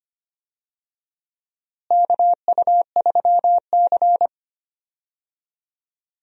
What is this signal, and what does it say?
Morse code, heard as a keyed tone, about 25 wpm, 700 Hz. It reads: KU3C
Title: USA Call Signs - 4 Characters - Repeated 3x - 25wpm